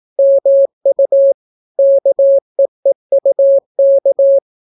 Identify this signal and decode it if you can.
Morse code, heard as a keyed tone, about 18 wpm, 550 Hz. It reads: MU KEEUK